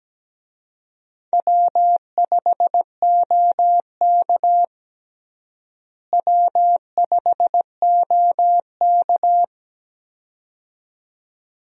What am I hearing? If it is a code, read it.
Morse code, heard as a keyed tone, about 17 wpm, 700 Hz. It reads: W5OK W5OK
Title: USA Call Signs - 4 Characters - Repeated 2x - 17wpm